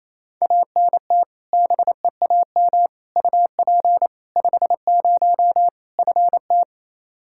Morse code, heard as a keyed tone, about 28 wpm, 700 Hz. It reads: ANT BEAM UP 50 FT
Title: Practice QSO - Exchange 2 - Info - 28wpm